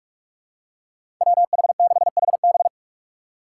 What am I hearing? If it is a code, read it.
Morse code, heard as a keyed tone, about 45 wpm, 700 Hz. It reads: W H 6 H B
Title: USA Call Signs - 5 Characters - 6x Character Spacing - 45wpm